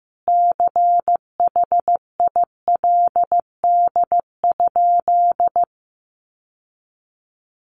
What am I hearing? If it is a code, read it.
Morse code, heard as a keyed tone, about 15 wpm, 700 Hz. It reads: CHILD?
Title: Sentences from Top 800 Words 15wpm